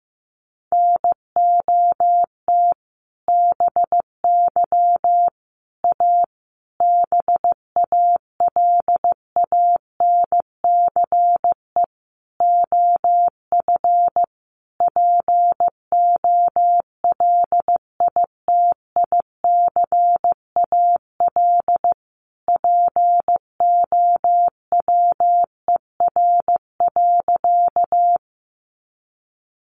Morse code, heard as a keyed tone, about 15 wpm, 700 Hz. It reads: NOT BY A BALANCE OF POLITICAL POWER.
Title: Sentences from Top 900 Words 15wpm